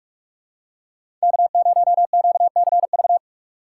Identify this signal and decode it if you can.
Morse code, heard as a keyed tone, about 42 wpm, 700 Hz. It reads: K0QCV
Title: USA Call Signs - 3x Character Spacing - 45wpm